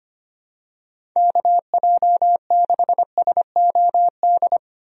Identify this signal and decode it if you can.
Morse code, heard as a keyed tone, about 25 wpm, 700 Hz. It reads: KJ6SOD
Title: USA Call Signs - 6 Characters - 25wpm